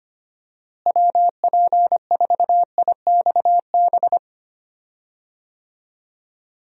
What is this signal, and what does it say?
Morse code, heard as a keyed tone, about 25 wpm, 700 Hz. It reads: WP4IXB
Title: USA Call Signs - 6 Characters - Repeated 2x - 25wpm